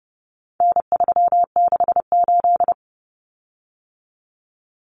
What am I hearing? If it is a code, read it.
Morse code, heard as a keyed tone, about 30 wpm, 700 Hz. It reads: N368
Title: Common Cut Numbers - 4-Digit 30wpm